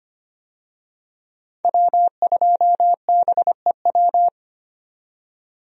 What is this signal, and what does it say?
Morse code, heard as a keyed tone, about 25 wpm, 700 Hz. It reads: W2BEW
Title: USA Call Signs - 5 Characters - 25wpm